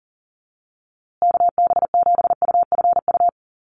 Morse code, heard as a keyed tone, about 40 wpm, 700 Hz. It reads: KB7UFU
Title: USA Call Signs Repeated 3x 40wpm